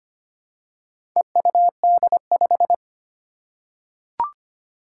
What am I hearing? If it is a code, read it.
Morse code, heard as a keyed tone, about 25 wpm, 700 Hz. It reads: EUD5
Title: Four Challenging Letter-Number Combinations 25wpm